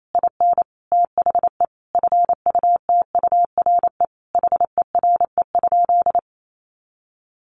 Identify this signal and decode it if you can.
Morse code, heard as a keyed tone, about 28 wpm, 700 Hz. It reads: IN THE FUTURE HERE?
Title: Sentences from Top 1500 Words 28wpm